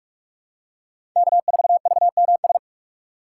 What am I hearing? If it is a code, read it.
Morse code, heard as a keyed tone, about 43 wpm, 700 Hz. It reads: K4VMS
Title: USA Call Signs - 4x Character Spacing - 45wpm